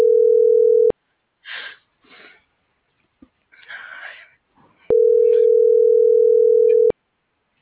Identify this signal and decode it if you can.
Ringback tone (North America), 440 + 480 Hz.